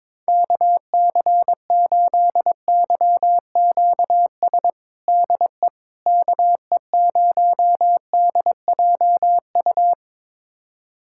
Morse code, heard as a keyed tone, about 21 wpm, 700 Hz. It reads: KC8YQS DE KE0DJU
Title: USA Callsign DE Callsign 22wpm